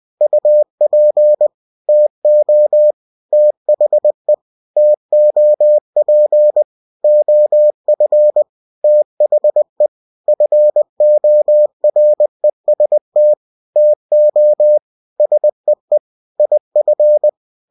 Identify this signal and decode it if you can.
Morse code, heard as a keyed tone, about 20 wpm, 600 Hz. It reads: UP TO THE TOP OF THE FOREST TO SEE IF